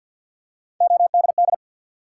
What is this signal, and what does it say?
Morse code, heard as a keyed tone, about 52 wpm, 700 Hz. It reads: ODD
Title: Three Letter Words - 50wpm